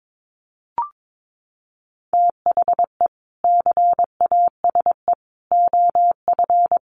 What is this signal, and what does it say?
Morse code, heard as a keyed tone, about 22 wpm, 700 Hz. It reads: THE CASE OF
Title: Sets of 3 Words 22wpm